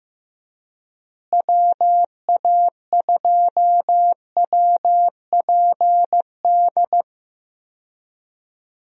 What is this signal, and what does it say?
Morse code, heard as a keyed tone, about 15 wpm, 700 Hz. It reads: WA2WPD
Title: USA Call Signs - 6 Characters - Repeated 2x - 15wpm